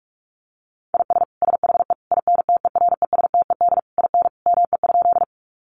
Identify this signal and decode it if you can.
Morse code, heard as a keyed tone, about 44 wpm, 700 Hz. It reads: IS SHE INTERESTED IN ME?
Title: Sentences from Top 900 Words 45wpm